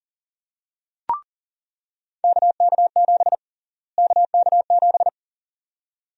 Morse code, heard as a keyed tone, about 40 wpm, 700 Hz. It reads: KK7 KK7
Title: USA Call Sign Prefixes - Repeated 2x - 40wpm